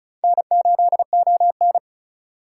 Morse code, heard as a keyed tone, about 34 wpm, 700 Hz. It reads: N8ON
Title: USA Call Signs - 4 Characters - 4x Character Spacing - 35wpm